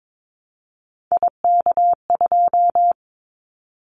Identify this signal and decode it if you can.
Morse code, heard as a keyed tone, about 22 wpm, 700 Hz. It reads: IK2
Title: Callsign Prefixes for Top 25 DXCC - 22wpm